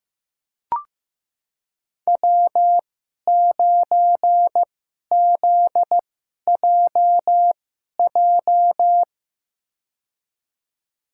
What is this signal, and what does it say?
Morse code, heard as a keyed tone, about 15 wpm, spaced as for 11 wpm, 700 Hz. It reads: W9ZJJ
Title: USA Call Signs - 5 Characters - 2x Character Spacing - 15wpm